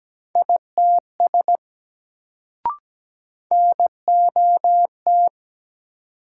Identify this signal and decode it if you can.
Morse code, heard as a keyed tone, about 17 wpm, 700 Hz. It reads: ITS NOT
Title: Sets of 2 Words - New Word Drill - 17wpm